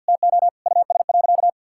Morse code, heard as a keyed tone, about 52 wpm, 700 Hz. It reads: TO US.
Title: Sentences from Top 1600 Words 50wpm